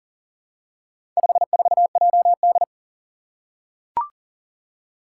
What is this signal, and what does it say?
Morse code, heard as a keyed tone, about 40 wpm, 700 Hz. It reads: 54JD
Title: Four Challenging Letter-Number Combinations 40wpm